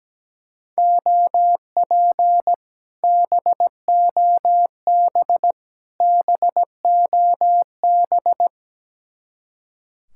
Morse code, heard as a keyed tone, about 17 wpm, 700 Hz. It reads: OP BOB BOB
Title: Speed-Racing Common QSO Phrases - 17wpm